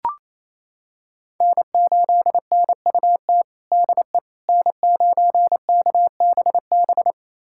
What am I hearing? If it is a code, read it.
Morse code, heard as a keyed tone, about 27 wpm, 700 Hz. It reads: N8NUT DE N9KBB
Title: USA Callsign DE Callsign - 5 Characters - 28wpm